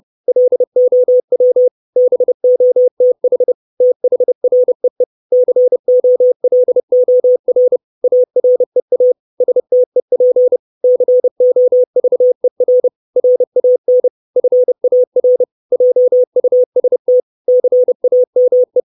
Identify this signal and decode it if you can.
Morse code, heard as a keyed tone, about 30 wpm, 500 Hz. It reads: LOW BOTH THREE COLOR AREA STEP COVER RAN FAR JUST CAME
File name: 632 words / 632 Words 30WPM